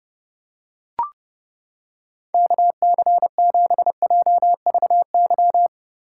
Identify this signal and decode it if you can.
Morse code, heard as a keyed tone, about 31 wpm, 700 Hz. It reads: KC7JVY